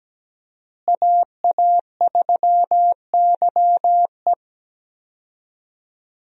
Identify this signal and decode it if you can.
Morse code, heard as a keyed tone, about 17 wpm, 700 Hz. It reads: AA3YE
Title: USA Call Signs - 5 Characters - Repeated 3x - 17wpm